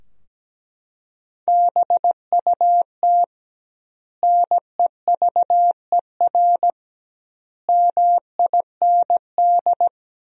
Morse code, heard as a keyed tone, about 17 wpm, 700 Hz. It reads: BUT NEVER MIND
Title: 2x Word-Spacing - Sentences from Top 500 Words - 17wpm